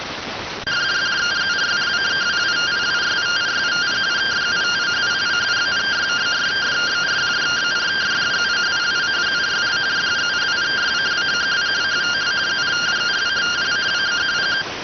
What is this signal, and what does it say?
Two-tone teleprinter signal, about 170 Hz shift.